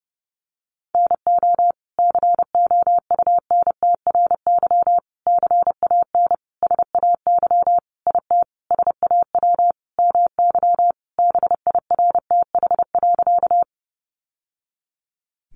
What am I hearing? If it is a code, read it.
Morse code, heard as a keyed tone, about 30 wpm, 700 Hz. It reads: NO COUNTRY CAN SAY IT SAW MY BIRTH.
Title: Sentences from Top 800 Words 30wpm